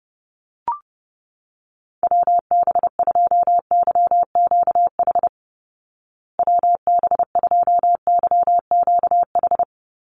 Morse code, heard as a keyed tone, about 30 wpm, 700 Hz. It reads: WB2YQH WB2YQH
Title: Call Signs Repeated 2x - 30wpm